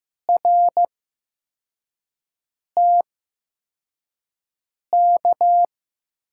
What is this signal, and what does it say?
Morse code, heard as a keyed tone, about 15 wpm, spaced as for 4 wpm, 700 Hz. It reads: RTK
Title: USA Call Signs - 6 Characters - 8x Character Spacing - 15wpm